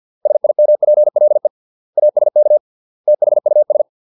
Morse code, heard as a keyed tone, about 50 wpm, 600 Hz. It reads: SIMPLE ASK THUS